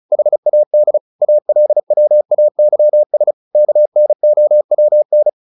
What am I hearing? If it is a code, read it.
Morse code, heard as a keyed tone, about 34 wpm, 600 Hz. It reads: HAD ALWAYS KNOWN
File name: the old man and the sea 2 35wpm